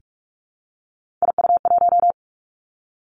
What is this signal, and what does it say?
Morse code, heard as a keyed tone, about 46 wpm, 700 Hz. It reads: IU1
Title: Three Challenging Letter-Number Combinations 45wpm